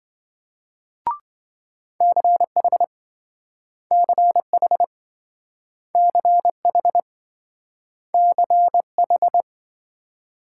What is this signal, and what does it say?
Morse code, heard as a keyed tone, about 27 wpm, 700 Hz. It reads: CH CH CH CH